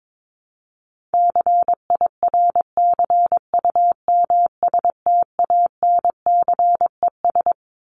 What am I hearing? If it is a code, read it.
Morse code, heard as a keyed tone, about 22 wpm, 700 Hz. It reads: CIRCUMSTANCES
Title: ES:S Suffix - Top 100 Words with Base Words 22wpm